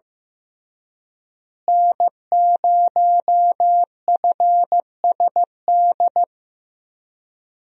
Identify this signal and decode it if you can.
Morse code, heard as a keyed tone, about 16 wpm, 700 Hz. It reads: N0FSD
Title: USA Call Signs - 5 Characters - Repeated 3x - 15wpm